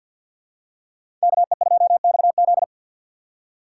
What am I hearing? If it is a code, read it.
Morse code, heard as a keyed tone, about 53 wpm, 700 Hz. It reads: KE2XZ
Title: USA Call Signs - 6x Character Spacing - 50wpm